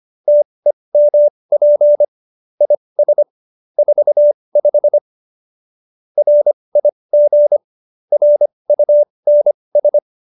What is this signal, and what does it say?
Morse code, heard as a keyed tone, about 25 wpm, spaced as for 20 wpm, 590 Hz. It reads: TEMP IS 45 RIG RUNS